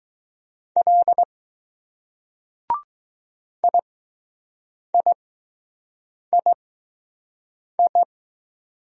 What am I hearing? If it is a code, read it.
Morse code, heard as a keyed tone, about 20 wpm, spaced as for 6 wpm, 700 Hz. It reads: L IIII